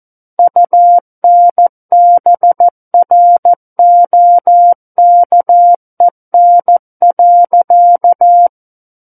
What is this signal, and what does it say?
Morse code, heard as a keyed tone, about 14 wpm, 700 Hz. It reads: UNBROKEN.
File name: Book One / 05-Chapter 5 - 14 WPM